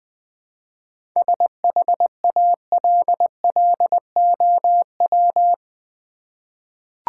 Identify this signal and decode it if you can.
Morse code, heard as a keyed tone, about 20 wpm, 700 Hz. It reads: SHALLOW
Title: 7 Letter Words - First 3 Letters Given - 20wpm